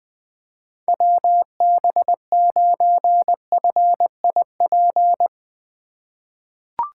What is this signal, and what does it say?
Morse code, heard as a keyed tone, about 21 wpm, 700 Hz. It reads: WB9FIP
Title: K1USN SST QSO Practice - 20wpm